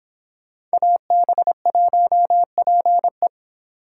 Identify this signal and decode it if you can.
Morse code, heard as a keyed tone, about 26 wpm, 700 Hz. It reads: AB1PE